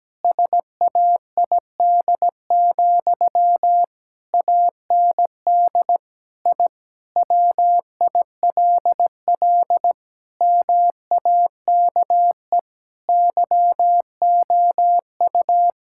Morse code, heard as a keyed tone, about 17 wpm, 700 Hz. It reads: SAID, AND I WILL MAKE YOU
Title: Sentences from Top 1000 Words - Review Part 6 - 17wpm